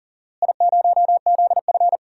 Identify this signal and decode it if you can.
Morse code, heard as a keyed tone, about 38 wpm, 700 Hz. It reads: I0ZF
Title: DXCC-100 Call Signs - 4 Characters - Repeated 2x - 40wpm